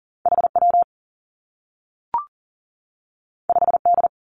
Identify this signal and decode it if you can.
Morse code, heard as a keyed tone, about 41 wpm, 700 Hz. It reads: H W 5 D
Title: Two Challenging Letter-Number Combinations 40wpm